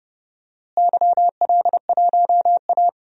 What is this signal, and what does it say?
Morse code, heard as a keyed tone, about 30 wpm, 700 Hz. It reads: YL1A
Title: DXCC-100 Call Signs - 4 Characters - Repeated 3x - 30wpm